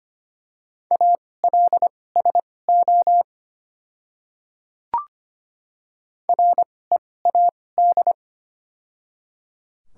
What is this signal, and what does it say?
Morse code, heard as a keyed tone, about 25 wpm, spaced as for 18 wpm, 700 Hz. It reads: ALSO READ
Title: 2x Character Spacing - Top 200 Words - 25wpm